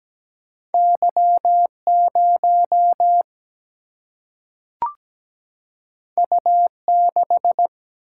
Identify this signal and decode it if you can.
Morse code, heard as a keyed tone, about 17 wpm, 700 Hz. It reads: Y0 U6
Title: Single-Letter Single-Number 17wpm